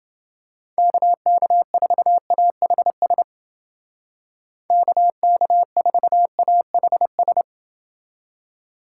Morse code, heard as a keyed tone, about 29 wpm, 700 Hz. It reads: KK4AHS KK4AHS